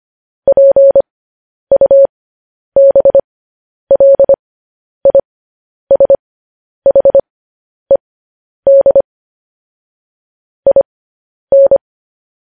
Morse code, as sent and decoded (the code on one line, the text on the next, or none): .--. ..- -... .-.. .. ... .... -.. / .. -.
PUBLISHED IN